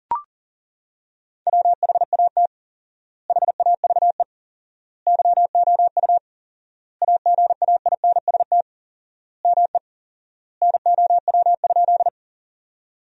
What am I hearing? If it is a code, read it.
Morse code, heard as a keyed tone, about 40 wpm, 700 Hz. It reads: WHAT HAVE YOU AGAINST ME NOW?
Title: 4x Word-Spacing - Sentences from Top 400 Words - 40wpm